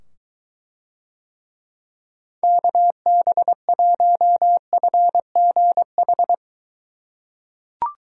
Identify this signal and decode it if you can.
Morse code, heard as a keyed tone, about 23 wpm, 700 Hz. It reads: KB1FGH